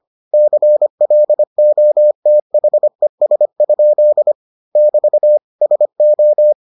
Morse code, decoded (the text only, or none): CLOTHES? <BT> SO